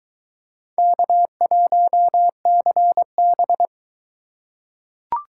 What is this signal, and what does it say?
Morse code, heard as a keyed tone, about 20 wpm, 700 Hz. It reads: K1CB